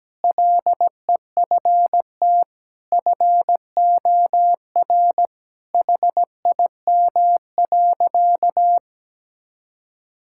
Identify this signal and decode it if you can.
Morse code, heard as a keyed tone, about 17 wpm, 700 Hz. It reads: LEFT FOR HIM.
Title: Sentences from Top 1000 Words - Review Part 6 - 17wpm